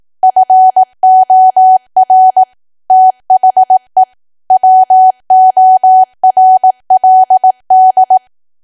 Morse code, heard as a keyed tone, about 18 wpm, 750 Hz. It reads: FOR THE WORLD
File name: Archive / 18 WPM / 18WPM